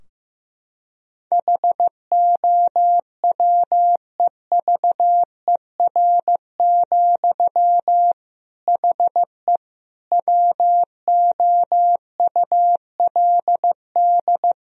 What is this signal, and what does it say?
Morse code, heard as a keyed tone, about 15 wpm, 700 Hz. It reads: HOWEVER, HE WOULD
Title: Sentences from Top 1000 Words - Review Part 2 - 15wpm